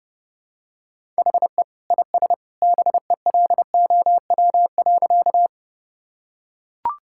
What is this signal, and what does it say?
Morse code, heard as a keyed tone, about 30 wpm, 700 Hz. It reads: HE IS BELOW.
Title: Sentences from Top 1000 Words - Review Part 5 - 30wpm